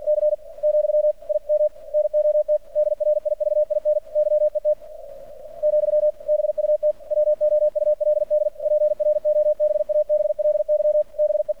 Morse code, heard as a keyed tone, about 48 wpm, 600 Hz. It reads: C , I M NOT DRIVIN YET , BUT WOULD PROBABLY BE
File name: archived / CRSnet